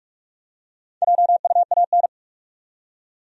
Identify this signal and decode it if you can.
Morse code, heard as a keyed tone, about 46 wpm, 700 Hz. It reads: JUAN
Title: USA Common Names - Max 4 Letters - 45wpm